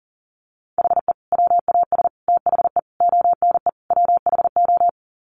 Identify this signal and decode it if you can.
Morse code, heard as a keyed tone, about 37 wpm, 700 Hz. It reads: HE WAS THE ONE WHO
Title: Sets of 5 Words Encore2 40wpm